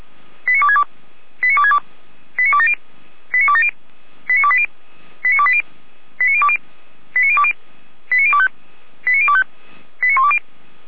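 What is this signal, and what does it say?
Five-tone selective call (ZVEI), heard as quick runs of five tones, tones of about 70 ms. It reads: 89262 89262 89270 89270 89280 89280 89020 89020 89025 89025 89120
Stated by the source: ZVEI five-tone selective call received on an SDR.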